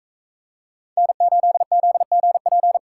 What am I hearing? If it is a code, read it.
Morse code, heard as a keyed tone, about 42 wpm, 700 Hz. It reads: N8ZGP